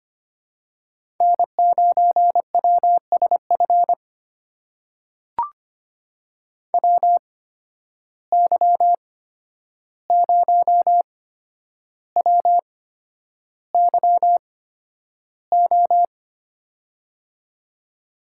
Morse code, heard as a keyed tone, about 25 wpm, spaced as for 10 wpm, 700 Hz. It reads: N9WSF WY0WYO
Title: USA Call Signs - 8x Character Spacing - 25wpm